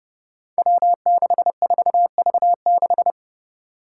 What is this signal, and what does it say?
Morse code, heard as a keyed tone, about 32 wpm, 700 Hz. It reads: W64V6
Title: Five Challenging Letter-Number Combinations 30wpm